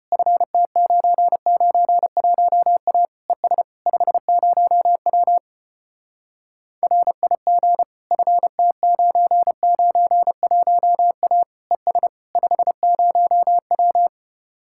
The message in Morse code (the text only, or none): FT991A ES 50W RIG FT991A ES 50W